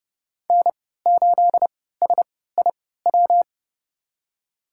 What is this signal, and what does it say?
Morse code, heard as a keyed tone, about 32 wpm, spaced as for 17 wpm, 700 Hz. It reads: N8SIW
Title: USA Call Signs - 5 Characters - 3x Character Spacing - 30wpm